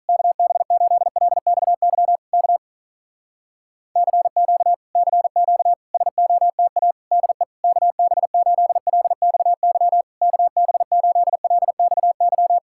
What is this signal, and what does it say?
Morse code, heard as a keyed tone, about 48 wpm, 700 Hz. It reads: KB8LXY K CQ CQ SOTA DE KB8LXY KB8LXY